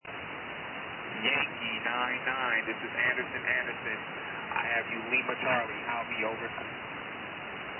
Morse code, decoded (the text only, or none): none